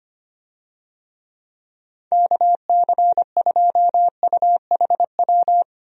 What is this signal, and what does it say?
Morse code, heard as a keyed tone, about 25 wpm, 700 Hz. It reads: KC2UHW
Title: USA Call Signs - 6 Characters - 25wpm